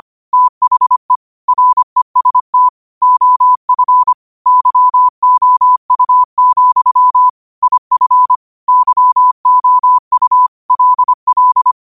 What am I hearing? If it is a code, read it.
Morse code, heard as a keyed tone, about 25 wpm, 1000 Hz. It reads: THE REST OF YOU, IF YOU LL
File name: qotdfiles / 25wpm